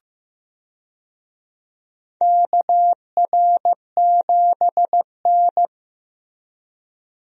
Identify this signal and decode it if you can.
Morse code, heard as a keyed tone, about 15 wpm, 700 Hz. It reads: KR7N
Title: USA Call Signs - 4 Characters - Repeated 2x - 15wpm